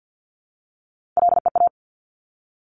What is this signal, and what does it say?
Morse code, heard as a keyed tone, about 50 wpm, 690 Hz. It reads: LEA
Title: Top Three Letter Combinations in Words 50wpm